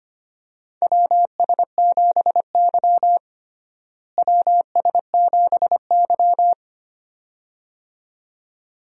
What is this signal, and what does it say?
Morse code, heard as a keyed tone, about 25 wpm, 700 Hz. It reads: WS7Y WS7Y
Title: USA Call Signs - 4 Characters - Repeated 2x - 25wpm